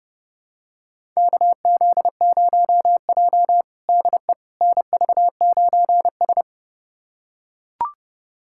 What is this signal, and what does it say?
Morse code, heard as a keyed tone, about 30 wpm, 700 Hz. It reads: KZ0J DE NV9S
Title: USA Callsign DE Callsign - 4 Characters - 30wpm